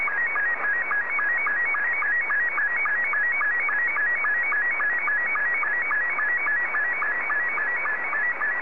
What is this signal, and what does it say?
SSTV (slow-scan television), Scottie 2.